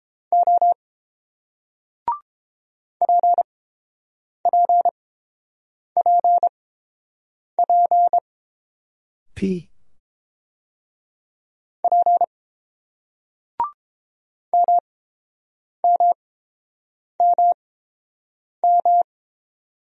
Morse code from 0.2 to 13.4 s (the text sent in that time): O PPPP P